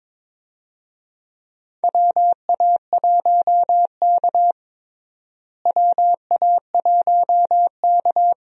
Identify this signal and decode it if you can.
Morse code, heard as a keyed tone, about 22 wpm, 700 Hz. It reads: WA1K WA1K
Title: USA Call Signs - 4 Characters - Repeated 3x - 22wpm